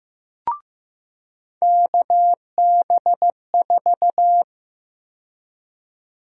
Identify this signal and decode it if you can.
Morse code, heard as a keyed tone, about 15 wpm, 700 Hz. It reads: KB4
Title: USA Call Sign Prefixes - 15wpm